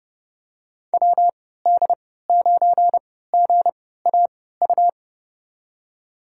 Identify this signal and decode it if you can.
Morse code, heard as a keyed tone, about 30 wpm, spaced as for 17 wpm, 700 Hz. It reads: WD9GAU
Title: USA Call Signs - 6 Characters - 3x Character Spacing - 30wpm